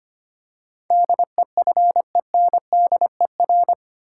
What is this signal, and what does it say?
Morse code, heard as a keyed tone, about 25 wpm, 700 Hz. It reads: DEFENDER
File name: Eight Letter Words - 25wpm